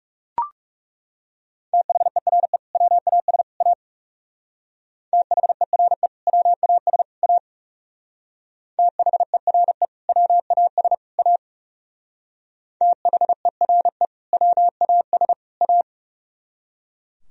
Morse code, heard as a keyed tone, about 37 wpm, 700 Hz. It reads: THERE WAS A THERE WAS A THERE WAS A THERE WAS A
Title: Speed-Racing - Sets of 3 Words 30wpm